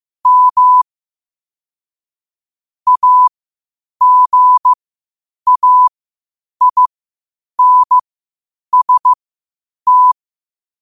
Morse code, heard as a keyed tone, about 15 wpm, spaced as for 8 wpm, 1000 Hz. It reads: M AGAINST
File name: qotdfiles / 8wpm